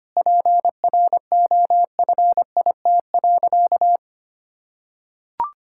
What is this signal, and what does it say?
Morse code, heard as a keyed tone, about 25 wpm, 700 Hz. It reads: PROFIT.
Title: Sentences from Top 1000 Words - Review Part 5 - 25wpm